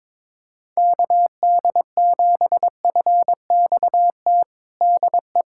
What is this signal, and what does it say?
Morse code, heard as a keyed tone, about 22 wpm, 700 Hz. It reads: KD7FXT DE
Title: USA Callsign DE Callsign - 6 Characters - 22wpm